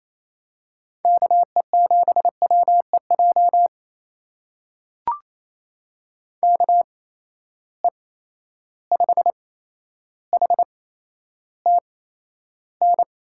Morse code, heard as a keyed tone, about 28 wpm, spaced as for 12 wpm, 700 Hz. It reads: KE7WEJ KE5HTN
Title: USA Call Signs - 6 Characters - 8x Character Spacing - 28wpm